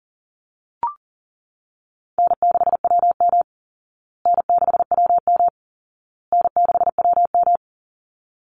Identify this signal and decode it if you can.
Morse code, heard as a keyed tone, about 40 wpm, 700 Hz. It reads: N6WM N6WM N6WM
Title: Easy Call Signs Repeated 3x - 40wpm